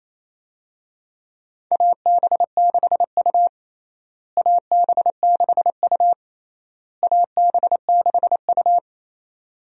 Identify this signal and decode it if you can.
Morse code, heard as a keyed tone, about 28 wpm, 700 Hz. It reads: AB6U AB6U AB6U
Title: USA Call Signs - 4 Characters - Repeated 3x - 28wpm